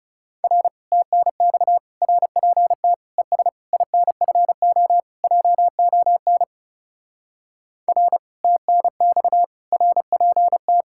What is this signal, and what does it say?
Morse code, heard as a keyed tone, about 40 wpm, 700 Hz. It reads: R TNX RPT ES INFO JON R TNX RPT